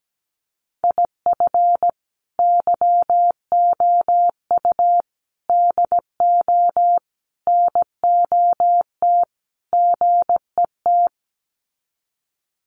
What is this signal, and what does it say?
Morse code, heard as a keyed tone, about 17 wpm, 700 Hz. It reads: IF YOU DO NOT GET
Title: Sets of 5 Words Encore2 17wpm